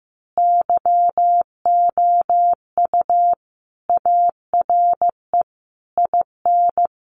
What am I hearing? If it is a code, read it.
Morse code, heard as a keyed tone, about 14 wpm, 700 Hz. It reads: YOU ARE IN
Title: Sentences from Top 1500 Words 15wpm